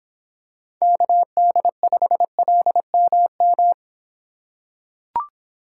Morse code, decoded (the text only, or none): KD5LMM